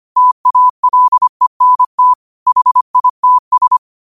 Morse code, heard as a keyed tone, about 25 wpm, 1000 Hz. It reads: TALENT HITS